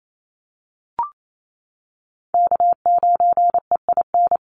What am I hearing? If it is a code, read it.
Morse code, heard as a keyed tone, about 28 wpm, 700 Hz. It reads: K9EIN
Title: USA Call Signs Repeated 2x 28wpm